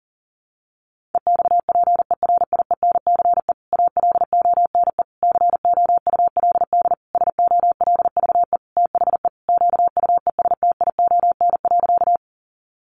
Morse code, heard as a keyed tone, about 40 wpm, 700 Hz. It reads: EXPERIENCE ALONE COULD SOLVE THE QUESTION.
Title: Sentences from Top 1000 Words 40wpm